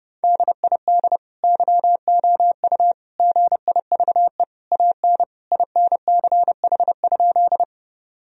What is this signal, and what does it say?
Morse code, heard as a keyed tone, about 30 wpm, 700 Hz. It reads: DID YOU GIVE AN INCH?
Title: Sentences from Top 1000 Words - Review Part 3 - 30wpm